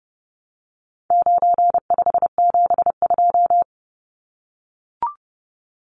Morse code, heard as a keyed tone, about 30 wpm, 700 Hz. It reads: 9572
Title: Numbers - 4-Digit 30wpm